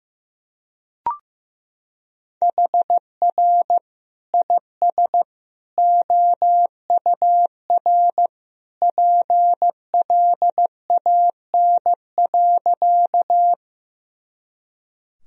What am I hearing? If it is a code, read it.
Morse code, heard as a keyed tone, about 15 wpm, 700 Hz. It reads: HR IS OUR PLAN.
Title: Sentences from Top 500 Words and Abbreviations 15wpm